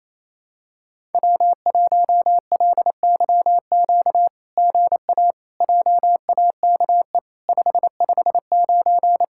Morse code, heard as a keyed tone, about 28 wpm, 700 Hz. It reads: W1LYQ GA JAKE 559
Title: SOTA QSO Practice 28wpm